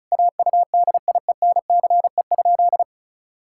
Morse code, heard as a keyed tone, about 36 wpm, 700 Hz. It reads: AUDIENCE?
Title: Sentences from Top 1000 Words - Review Part 5 - 35wpm